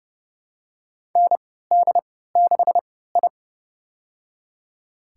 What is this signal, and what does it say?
Morse code, heard as a keyed tone, about 30 wpm, spaced as for 17 wpm, 700 Hz. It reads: ND6I
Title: USA Call Signs - 4 Characters - 3x Character Spacing - 30wpm